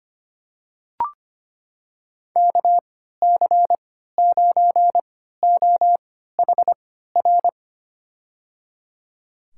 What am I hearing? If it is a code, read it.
Morse code, heard as a keyed tone, about 25 wpm, spaced as for 14 wpm, 700 Hz. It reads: KC9OHR
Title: USA Call Signs - 6 Characters - 3x Character Spacing - 25wpm